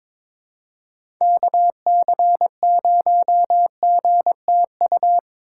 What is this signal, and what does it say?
Morse code, heard as a keyed tone, about 22 wpm, 700 Hz. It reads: KC0GTU
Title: USA Call Signs - 6 Characters - 3x Character Spacing - 22wpm